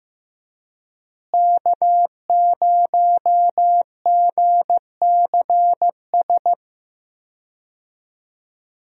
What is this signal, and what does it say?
Morse code, heard as a keyed tone, about 15 wpm, 700 Hz. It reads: K0GCS
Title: USA Call Signs - 5 Characters - Repeated 2x - 15wpm